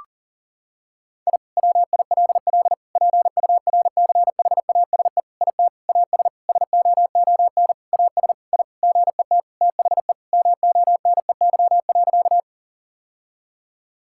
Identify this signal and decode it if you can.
Morse code, heard as a keyed tone, about 40 wpm, 700 Hz. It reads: I WILL PURCHASE IT AS SOON AS I GET THE MONEY.
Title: Sentences from Top 1000 Words - Review Part 5 - 40wpm